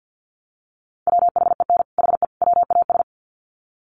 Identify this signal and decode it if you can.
Morse code, heard as a keyed tone, about 49 wpm, 700 Hz. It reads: WHEN HE WAS
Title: Sets of 3 Words 50wpm